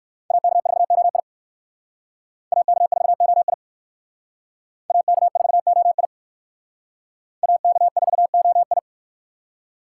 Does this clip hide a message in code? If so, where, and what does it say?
Yes: Morse code from 0.3 to 8.8 s, heard as a keyed tone, about 63 wpm, 700 Hz. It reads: AK4OI AK4OI AK4OI AK4OI